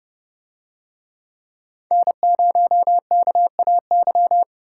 Morse code, heard as a keyed tone, about 28 wpm, 700 Hz. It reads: N0KAY